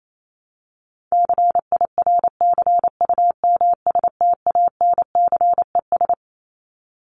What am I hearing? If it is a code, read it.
Morse code, heard as a keyed tone, about 28 wpm, 700 Hz. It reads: CIRCUMSTANCES
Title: ES:S Suffix - Top 100 Words with Base Words 28wpm